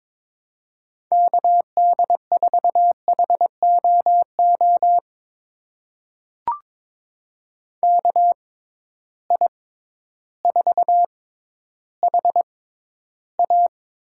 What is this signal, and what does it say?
Morse code, heard as a keyed tone, about 22 wpm, spaced as for 12 wpm, 700 Hz. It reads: KD4HOO KI4HA
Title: USA Call Signs - 6 Characters - 6x Character Spacing - 22wpm